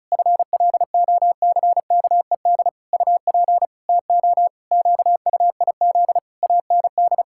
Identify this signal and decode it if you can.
Morse code, heard as a keyed tone, about 36 wpm, 700 Hz. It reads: FLOCKED UP TO QUIZ AND
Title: Pangram Challenge v2 - 35wpm